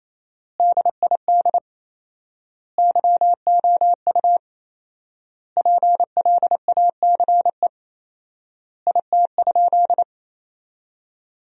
Morse code, heard as a keyed tone, about 28 wpm, 700 Hz. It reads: DID YOU PLACE IT?
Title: 4x Word-Spacing - Sentences from Top 200 Words - 28wpm